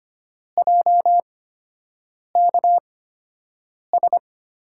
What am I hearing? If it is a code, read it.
Morse code, heard as a keyed tone, about 25 wpm, spaced as for 7 wpm, 700 Hz. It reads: JKS
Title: USA Call Signs - 5 Characters - 8x Character Spacing - 25wpm